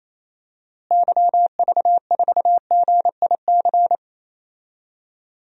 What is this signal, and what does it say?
Morse code, heard as a keyed tone, about 28 wpm, 700 Hz. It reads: YV4GIC
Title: DXCC-100 Call Signs - 6 Characters - Repeated 3x - 28wpm